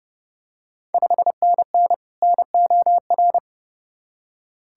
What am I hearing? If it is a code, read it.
Morse code, heard as a keyed tone, about 31 wpm, 700 Hz. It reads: 5NN NOR
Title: Ontario QSO Party Exchange 30wpm